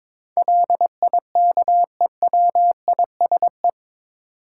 Morse code, heard as a keyed tone, about 22 wpm, 700 Hz. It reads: LIKEWISE
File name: practice / 22wpm / Eight Letter Words - 22wpm